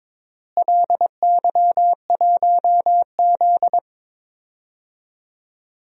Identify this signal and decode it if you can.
Morse code, heard as a keyed tone, about 21 wpm, 700 Hz. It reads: LY1Z
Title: DXCC-100 Call Signs - 4 Characters - Repeated 2x - 22wpm